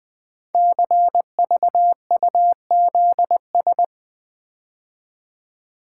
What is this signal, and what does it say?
Morse code, heard as a keyed tone, about 20 wpm, 700 Hz. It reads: CVUZS
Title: Code Groups - 5 Letters - 20wpm